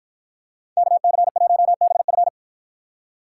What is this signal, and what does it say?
Morse code, heard as a keyed tone, about 49 wpm, 700 Hz. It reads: KC1BF